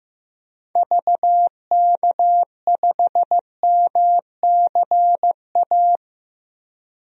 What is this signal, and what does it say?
Morse code, heard as a keyed tone, about 15 wpm, 690 Hz. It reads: VK5MCA
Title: DXCC-100 Call Signs - 6 Characters - 15wpm